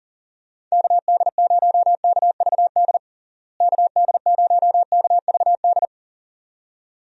Morse code, heard as a keyed tone, about 40 wpm, 700 Hz. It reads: KD0KVD KD0KVD
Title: USA Call Signs - 6 Characters - Repeated 2x - 40wpm